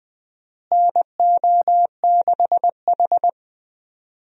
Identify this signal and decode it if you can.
Morse code, heard as a keyed tone, about 20 wpm, 700 Hz. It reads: NO6H
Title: USA Call Signs - 4 Characters - Repeated 2x - 20wpm